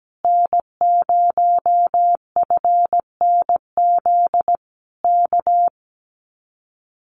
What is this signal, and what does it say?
Morse code, heard as a keyed tone, about 17 wpm, 700 Hz. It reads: N0FNZ K
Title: Common QSO Phrases Encore 17wpm